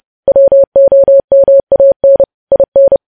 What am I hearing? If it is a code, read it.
Morse code, heard as a keyed tone, about 30 wpm, 550 Hz. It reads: WOMAN IN
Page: news headlines in morse code at 30 wpm